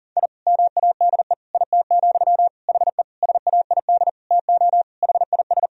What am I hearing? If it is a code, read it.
Morse code, heard as a keyed tone, about 37 wpm, 700 Hz. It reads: I MADE IT, HE SAID TO HIS
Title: Sentences from Top 1000 Words - Review Part 1 - 40wpm